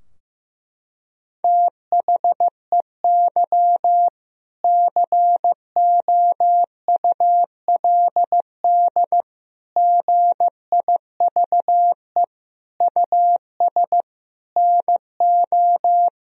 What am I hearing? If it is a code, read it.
Morse code, heard as a keyed tone, about 15 wpm, 700 Hz. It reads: THEY COULD GIVE US NO
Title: Sentences from Top 1000 Words - Review Part 3 - 15wpm